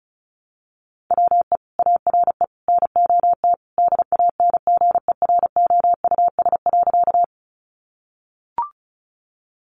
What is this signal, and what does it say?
Morse code, heard as a keyed tone, about 34 wpm, 700 Hz. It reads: WE ARE NOT DANGEROUS.
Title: Sentences from Top 1200 Words 35wpm